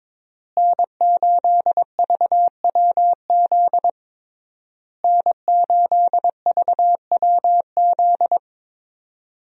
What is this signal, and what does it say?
Morse code, heard as a keyed tone, about 22 wpm, 700 Hz. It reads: N8VWZ N8VWZ
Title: USA Call Signs - 5 Characters - Repeated 3x - 22wpm